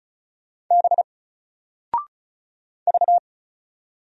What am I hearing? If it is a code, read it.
Morse code, heard as a keyed tone, about 36 wpm, 700 Hz. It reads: B V
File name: Single-Letter-Number-35wpm